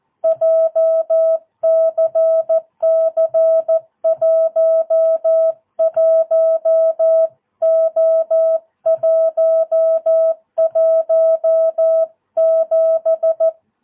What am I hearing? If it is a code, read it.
Morse code, heard as a keyed tone, about 14 wpm, 650 Hz. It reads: JCC 11O117
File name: jh8cbh / 2022cw